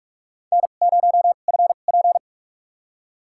Morse code, heard as a keyed tone, about 42 wpm, spaced as for 31 wpm, 700 Hz. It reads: N0FP
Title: USA Call Signs - 4 Characters - 2x Character Spacing - 45wpm